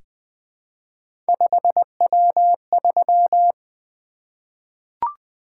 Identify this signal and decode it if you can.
Morse code, heard as a keyed tone, about 20 wpm, 700 Hz. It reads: 5W3
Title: Three Letter-Number Combinations - 20wpm